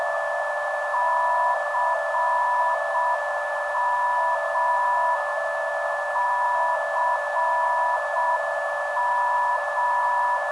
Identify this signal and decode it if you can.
Morse code, heard as a keyed tone, about 6 wpm, 980 Hz, with noise. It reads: CM CM
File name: ndb zvok / CM - 488.61kHz